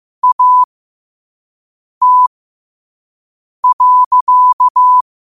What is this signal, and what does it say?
Morse code, heard as a keyed tone, about 15 wpm, spaced as for 5 wpm, 1000 Hz. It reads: AT.
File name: qotdfiles / 5wpm